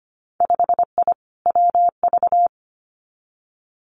Morse code, heard as a keyed tone, about 25 wpm, 700 Hz. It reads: 5I WV
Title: Winter Field Day Exchanges Encore 25wpm